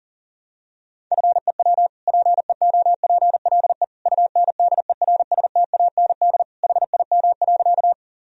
Morse code, heard as a keyed tone, about 40 wpm, 700 Hz. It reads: FEW PEOPLE UNDERSTAND HIM.
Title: Sentences from Top 1000 Words - Review Part 1 - 40wpm